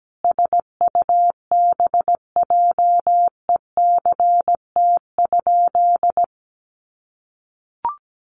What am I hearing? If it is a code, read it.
Morse code, heard as a keyed tone, about 17 wpm, 700 Hz. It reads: SUBJECT?
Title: Sentences from Top 800 Words 17wpm